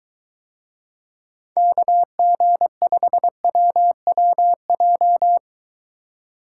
Morse code, heard as a keyed tone, about 23 wpm, 700 Hz. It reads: KG5WWJ